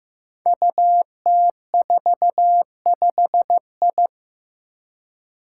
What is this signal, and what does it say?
Morse code, heard as a keyed tone, about 15 wpm, 700 Hz. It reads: UT45I
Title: Five Challenging Letter-Number Combinations 15wpm